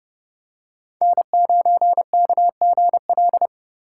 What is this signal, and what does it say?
Morse code, heard as a keyed tone, about 30 wpm, 700 Hz. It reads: N9KGL
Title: USA Call Signs - 5 Characters - 3x Character Spacing - 30wpm